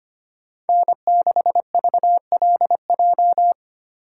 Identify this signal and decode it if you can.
Morse code, heard as a keyed tone, about 32 wpm, 700 Hz. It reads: N6VLJ